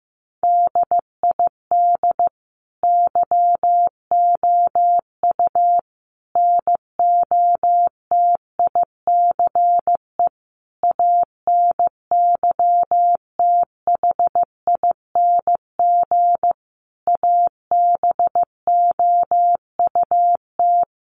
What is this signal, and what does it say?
Morse code, heard as a keyed tone, about 15 wpm, 700 Hz. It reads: DID YOU NOTICE ANYTHING ABOUT
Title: Sentences from Top 1000 Words 15wpm